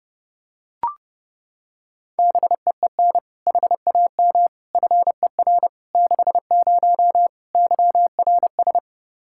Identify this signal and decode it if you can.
Morse code, heard as a keyed tone, about 32 wpm, 700 Hz. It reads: BEEN HAM FER 60 YRS
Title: K1USN SST QSO Practice 30wpm